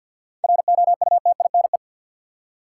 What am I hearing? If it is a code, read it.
Morse code, heard as a keyed tone, about 47 wpm, 700 Hz. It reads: ROUTINE
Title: Seven Letter Words - 50wpm